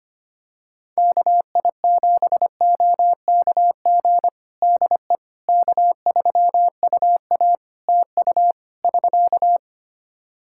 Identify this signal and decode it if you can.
Morse code, heard as a keyed tone, about 25 wpm, 700 Hz. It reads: KI7OKG DE K3UA TU <SK>
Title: Practice QSO - Exchange 4 - Ending - 25wpm